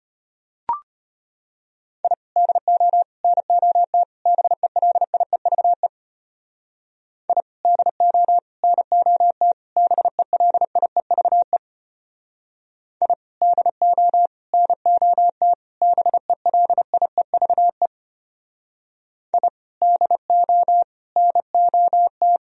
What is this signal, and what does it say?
Morse code, heard as a keyed tone, about 32 wpm, 700 Hz. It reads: I DO NOT BELIEVE I DO NOT BELIEVE I DO NOT BELIEVE I DO NOT